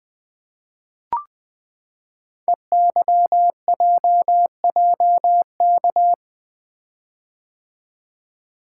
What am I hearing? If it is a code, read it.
Morse code, heard as a keyed tone, about 20 wpm, 700 Hz. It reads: EYJJK
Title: Code Groups - 5 Letters - 20wpm